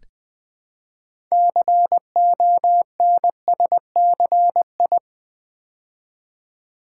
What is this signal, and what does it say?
Morse code, heard as a keyed tone, about 20 wpm, 700 Hz. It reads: CONSCI
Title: Build A Word - 3 Letters At A Time - 9-Letter Words - 20wpm